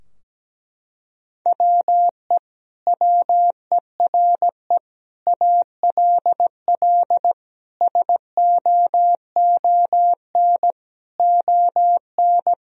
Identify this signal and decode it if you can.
Morse code, heard as a keyed tone, about 17 wpm, 700 Hz. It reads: WE WERE ALL SOON ON
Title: Sentences from Top 1000 Words - Review Part 4 - 17wpm